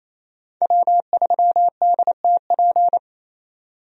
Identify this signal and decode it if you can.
Morse code, heard as a keyed tone, about 27 wpm, 700 Hz. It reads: W3DTP
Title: USA Call Signs - 5 Characters - 28wpm